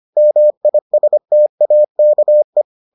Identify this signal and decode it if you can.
Morse code, heard as a keyed tone, about 25 wpm, 600 Hz. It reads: MISTAKE